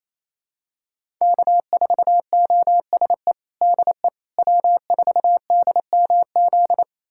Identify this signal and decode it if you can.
Morse code, heard as a keyed tone, about 28 wpm, 700 Hz. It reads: K4OSE DE W4DMZ
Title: USA Callsign DE Callsign - 5 Characters - 28wpm